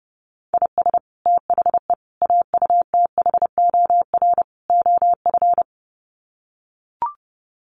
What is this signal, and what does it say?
Morse code, heard as a keyed tone, about 30 wpm, 700 Hz. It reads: IS THE AUTHOR OF
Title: Sets of 4 Words 30wpm